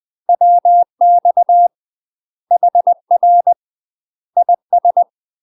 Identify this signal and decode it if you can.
Morse code, heard as a keyed tone, about 20 wpm, 700 Hz. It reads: WX HR IS
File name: UK MASTER QSOs TRAINING / 20WPM / UK MASTER QSOs 0003